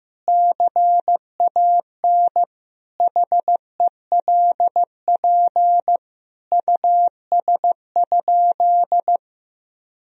Morse code, heard as a keyed tone, about 15 wpm, 700 Hz. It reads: CAN HELP US?
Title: Sentences from Top 1000 Words - Review Part 3 - 15wpm